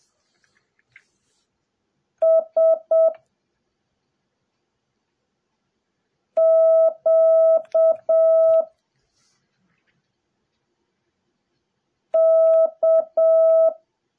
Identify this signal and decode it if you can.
Morse code, heard as a keyed tone, about 7 wpm, 650 Hz. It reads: S Q K
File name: jh8cbh / 2021cw / cw08a